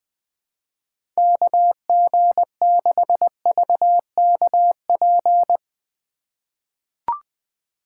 Morse code, heard as a keyed tone, about 21 wpm, 700 Hz. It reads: KG6VKP
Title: USA Call Signs - 3x Character Spacing - 20wpm